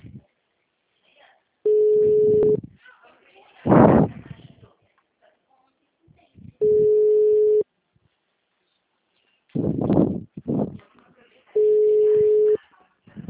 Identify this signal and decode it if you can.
Ringback tone (Europe, 425 Hz), with speech.